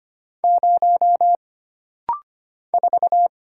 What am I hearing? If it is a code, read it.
Morse code, heard as a keyed tone, about 25 wpm, 700 Hz. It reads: 0 4
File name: Single Character (Letter:Number:Punctuation:Prosign) - Rapid-Fire - 25wpm